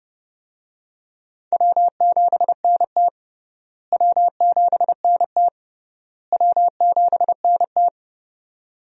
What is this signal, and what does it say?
Morse code, heard as a keyed tone, about 32 wpm, 700 Hz. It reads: W7NT W7NT W7NT
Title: USA Call Signs - 4 Characters - Repeated 3x - 30wpm